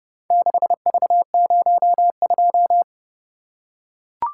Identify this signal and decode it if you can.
Morse code, heard as a keyed tone, about 30 wpm, 700 Hz. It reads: 6V02